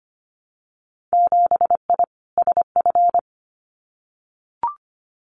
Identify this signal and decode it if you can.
Morse code, heard as a keyed tone, about 25 wpm, 700 Hz. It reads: 7I SF
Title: Winter Field Day Exchanges Encore 25wpm